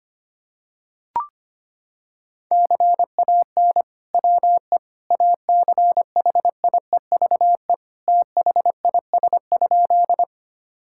Morse code, heard as a keyed tone, about 25 wpm, 700 Hz. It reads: CAN WE ACHIEVE THIS?
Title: Sentences from Top 1000 Words - Review Part 6 - 25wpm